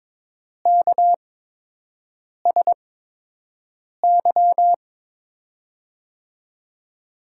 Morse code, heard as a keyed tone, about 22 wpm, spaced as for 6 wpm, 700 Hz. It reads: KSY
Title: USA Call Signs - 8x Character Spacing - 22wpm